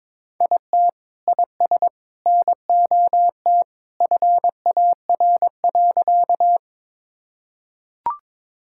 Morse code, heard as a keyed tone, about 22 wpm, 700 Hz. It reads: IT IS NOT FAR.
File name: practice / 22wpm / Sentences with 2-Word Phrases - 22wpm